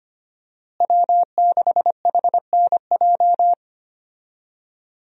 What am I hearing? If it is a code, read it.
Morse code, heard as a keyed tone, about 25 wpm, 700 Hz. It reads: W6HNJ
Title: USA Call Signs - 5 Characters - Repeated 3x - 25wpm